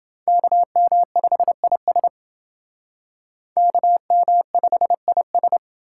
Morse code, heard as a keyed tone, about 29 wpm, 700 Hz. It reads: KM5IS KM5IS